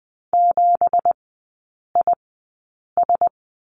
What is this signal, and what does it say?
Morse code, heard as a keyed tone, about 20 wpm, spaced as for 8 wpm, 700 Hz. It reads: 7IS
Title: 166 Lesson 27e - Number 7 - Three Characters 20wpm